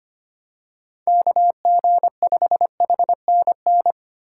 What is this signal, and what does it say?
Morse code, heard as a keyed tone, about 25 wpm, 700 Hz. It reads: KG5HNN
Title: USA Call Signs - 6 Characters - 3x Character Spacing - 25wpm